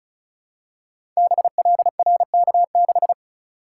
Morse code, heard as a keyed tone, about 36 wpm, 690 Hz. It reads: BLRK6
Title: Code Groups - 5 Letter:Numbers - 35wpm